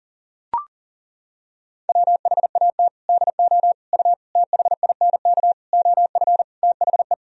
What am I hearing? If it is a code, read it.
Morse code, heard as a keyed tone, about 40 wpm, 700 Hz. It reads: WHAT DO U THINK OF THE
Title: Sentences from Top 300 Words and Abbreviations 40wpm